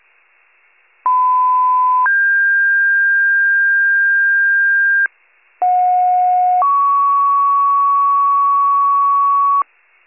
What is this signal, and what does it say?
Two-tone page, heard as a one-second tone, then a three-second tone, about 1010 then 1600 Hz, about 730 then 1120 Hz. It is heard from 1.1 to 9.6 s.
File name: 94 RESCUE DUTY